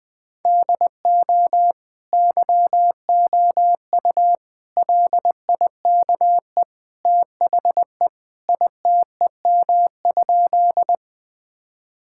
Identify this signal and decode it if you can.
Morse code, heard as a keyed tone, about 21 wpm, 700 Hz. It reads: DO YOU LIKE THE ITEM?
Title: Sentences from Top 1000 Words - Review Part 6 - 20wpm